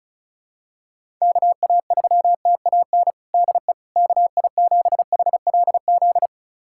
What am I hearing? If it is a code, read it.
Morse code, heard as a keyed tone, about 36 wpm, 700 Hz. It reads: KA3TAN DE KI7HLZ
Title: USA Callsign DE Callsign - 6 Characters - 35wpm